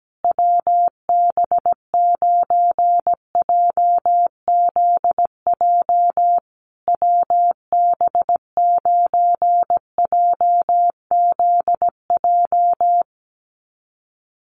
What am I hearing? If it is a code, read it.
Morse code, heard as a keyed tone, about 17 wpm, 700 Hz. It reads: WB9JZJ WB9JZJ
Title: USA Call Signs Repeated 3x 17wpm